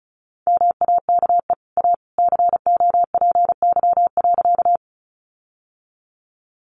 Morse code, heard as a keyed tone, about 34 wpm, 700 Hz. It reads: MAKE A COPY.
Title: Sentences from Top 1400 Words 35wpm